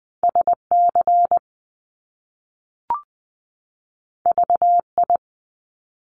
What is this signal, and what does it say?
Morse code, heard as a keyed tone, about 20 wpm, 700 Hz. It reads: SC VI
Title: Top Two Letter Combinations in Words 20wpm